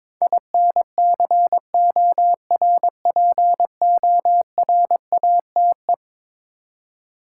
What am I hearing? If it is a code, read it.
Morse code, heard as a keyed tone, about 22 wpm, 700 Hz. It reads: INCORPORATE
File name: practice / 22wpm / Eleven Letter Words - 22wpm